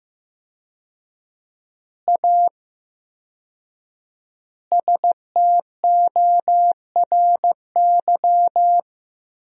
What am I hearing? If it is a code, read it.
Morse code, heard as a keyed tone, about 15 wpm, 700 Hz. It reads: A STORY
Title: 4x Word-Spacing - Sentences from Top 500 Words - 15wpm